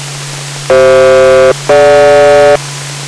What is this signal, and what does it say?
Aircraft SELCAL, heard as two pairs of tones. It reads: CF-DG